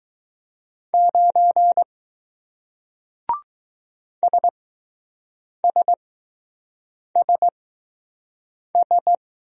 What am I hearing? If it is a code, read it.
Morse code, heard as a keyed tone, about 20 wpm, spaced as for 6 wpm, 700 Hz. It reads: 9 SSSS